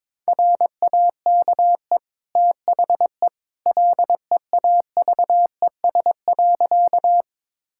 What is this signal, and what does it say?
Morse code, heard as a keyed tone, about 22 wpm, 700 Hz. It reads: RAKE THE LEAVES.
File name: practice / 22wpm / Easy Sentences with Hints - 22wpm